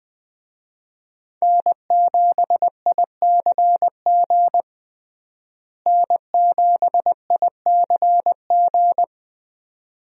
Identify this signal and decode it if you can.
Morse code, heard as a keyed tone, about 20 wpm, 700 Hz. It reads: N7ICG N7ICG
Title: USA Call Signs - 5 Characters - Repeated 2x - 20wpm